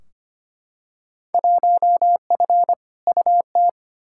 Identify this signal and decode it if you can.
Morse code, heard as a keyed tone, about 25 wpm, 700 Hz. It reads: 1F UT
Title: Field Day Exchanges Encore - 25wpm